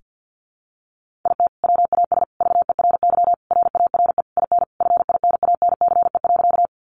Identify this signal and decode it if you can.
Morse code, heard as a keyed tone, about 50 wpm, 700 Hz. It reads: IT WAS VERY RARE IN FINANCE.
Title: Sentences from Top 1500 Words 50wpm